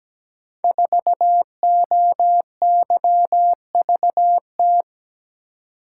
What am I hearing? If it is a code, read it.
Morse code, heard as a keyed tone, about 17 wpm, 700 Hz. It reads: 4OYVT
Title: Code Groups - 5 Letter:Numbers - 17wpm